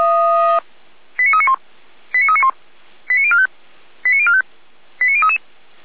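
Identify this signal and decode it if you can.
Five-tone selective call (ZVEI), heard as quick runs of five tones, tones of about 80 ms. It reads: 89391 89391 89045 89045 89033
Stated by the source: ZVEI five-tone call received by SDR, archived by Feuerwehr Orlamünde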